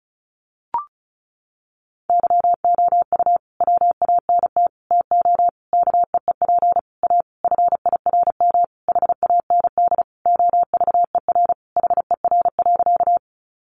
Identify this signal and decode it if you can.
Morse code, heard as a keyed tone, about 34 wpm, 700 Hz. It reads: YOU WANT TO KEEP A FIRM HAND OVER HER.
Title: Sentences from Top 1400 Words 35wpm